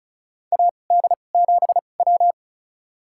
Morse code, heard as a keyed tone, about 36 wpm, spaced as for 25 wpm, 700 Hz. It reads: AD7W
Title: USA Call Signs - 4 Characters - 2x Character Spacing - 35wpm